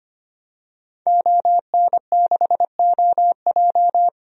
Morse code, heard as a keyed tone, about 25 wpm, 700 Hz. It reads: ON6OJ
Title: DXCC-100 Call Signs - 5 Characters - Repeated 3x - 25wpm